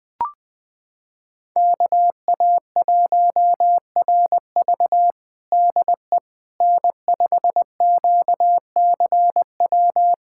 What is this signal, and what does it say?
Morse code, heard as a keyed tone, about 20 wpm, 700 Hz. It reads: KA1RV DE N5QCW
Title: USA Callsign DE Callsign - 5 Characters - 20wpm